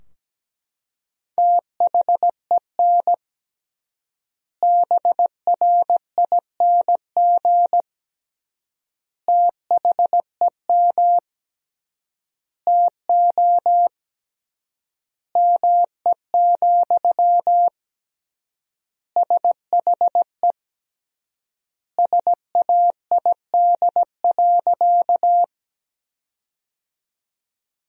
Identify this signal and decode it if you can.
Morse code, heard as a keyed tone, about 17 wpm, 700 Hz. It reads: THEN BRING THEM TO ME, SHE SAID.
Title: 3x Word-Spacing - Sentences from Top 500 Words - 17wpm